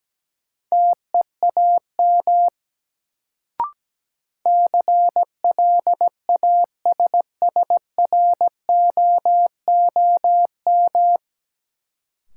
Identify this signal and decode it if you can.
Morse code, heard as a keyed tone, about 17 wpm, 700 Hz. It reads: TEAM CLASSROOM
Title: Sentences from Top 1500 Words - New Word Drill - 17wpm